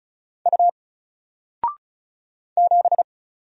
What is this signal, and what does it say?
Morse code, heard as a keyed tone, about 36 wpm, 700 Hz. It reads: U 7
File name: Single-Letter-Number-35wpm